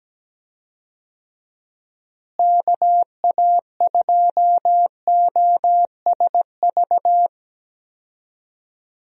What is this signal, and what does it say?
Morse code, heard as a keyed tone, about 17 wpm, 700 Hz. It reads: KA2OSV
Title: USA Call Signs - 6 Characters - Repeated 2x - 17wpm